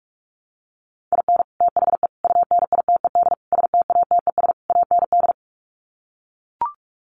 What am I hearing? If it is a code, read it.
Morse code, heard as a keyed tone, about 44 wpm, 700 Hz. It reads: IN THE UNITED STATES AND
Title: Sets of 5 Words 45wpm